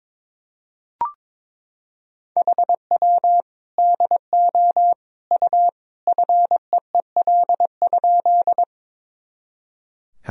Morse code, heard as a keyed tone, about 22 wpm, 700 Hz. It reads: HW DO U FEEL?
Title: Sentences from Top 400 Words and Abbreviations 22wpm